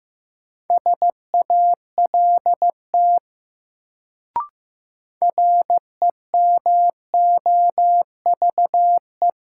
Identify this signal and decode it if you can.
Morse code, heard as a keyed tone, about 15 wpm, 700 Hz. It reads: SALT REMOVE
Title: Sentences from Top 800 Words - New Word Drill - 15wpm